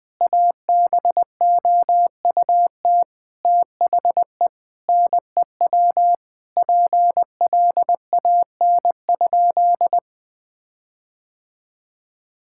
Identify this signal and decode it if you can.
Morse code, heard as a keyed tone, about 21 wpm, 700 Hz. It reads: ABOUT THE NEW PLAN?
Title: Sentences with 5-Word Phrases - 20wpm